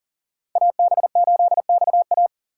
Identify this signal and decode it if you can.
Morse code, heard as a keyed tone, about 40 wpm, 700 Hz. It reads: AB8XA
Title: USA Call Signs - 5 Characters - 2x Character Spacing - 40wpm